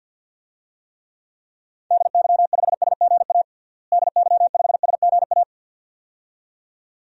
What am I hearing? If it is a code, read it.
Morse code, heard as a keyed tone, about 50 wpm, 700 Hz. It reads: DY5SGA DY5SGA
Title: DXCC-100 Call Signs - 6 Characters - Repeated 2x - 50wpm